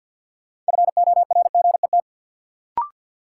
Morse code, heard as a keyed tone, about 53 wpm, 700 Hz. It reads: FORGET
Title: Sentences from Top 1200 Words - New Word Drill Alt - 50wpm